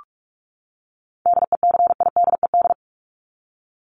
Morse code, heard as a keyed tone, about 44 wpm, 700 Hz. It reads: DECIDED
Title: ED Suffix - Top 100 Words with Base Words 45wpm